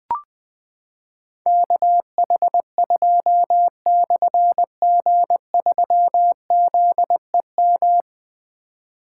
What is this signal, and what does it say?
Morse code, heard as a keyed tone, about 20 wpm, 700 Hz. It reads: KH2/G3ZEM
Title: Difficult Call Signs 20wpm